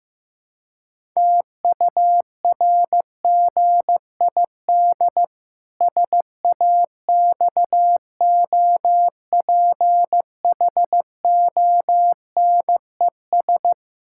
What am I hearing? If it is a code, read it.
Morse code, heard as a keyed tone, about 15 wpm, 700 Hz. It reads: TURGID SAXOPHONES
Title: Pangram Challenge v2 - 15wpm